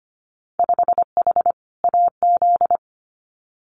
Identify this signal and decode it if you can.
Morse code, heard as a keyed tone, about 25 wpm, 700 Hz. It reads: 5H AZ
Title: Winter Field Day Exchanges Encore 25wpm